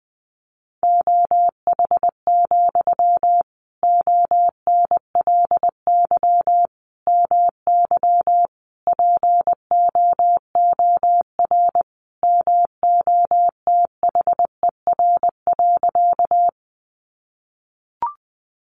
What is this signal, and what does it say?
Morse code, heard as a keyed tone, about 20 wpm, 700 Hz. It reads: OH, ONLY MY POOR MOTHER.
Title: Sentences from Top 1000 Words 20wpm